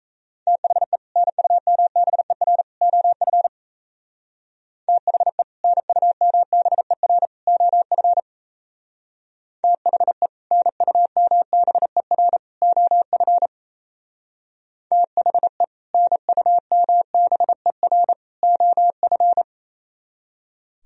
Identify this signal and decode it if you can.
Morse code, heard as a keyed tone, about 36 wpm, 700 Hz. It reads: THE NUMBER OF THE NUMBER OF THE NUMBER OF THE NUMBER OF